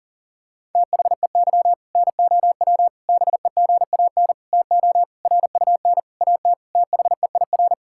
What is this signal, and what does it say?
Morse code, heard as a keyed tone, about 40 wpm, 700 Hz. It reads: THEY NOW BEGAN TO RUN AT THEIR
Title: Sentences from Top 1000 Words - Review Part 5 - 40wpm